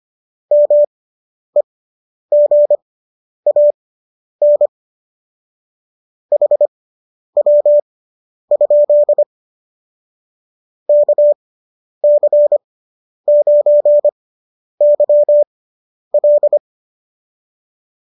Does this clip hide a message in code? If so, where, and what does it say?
Yes: Morse code from 0.5 to 16.6 s, heard as a keyed tone, about 25 wpm, spaced as for 10 wpm, 590 Hz. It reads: MEGAN HW? KC9YL